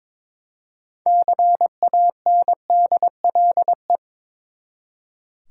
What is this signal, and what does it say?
Morse code, heard as a keyed tone, about 22 wpm, 700 Hz. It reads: CANDLE
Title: Six Letter Words - 22wpm